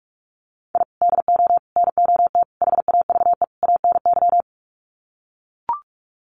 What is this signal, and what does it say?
Morse code, heard as a keyed tone, about 44 wpm, 700 Hz. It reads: I DO NOT HAVE ANY
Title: Sets of 5 Words 45wpm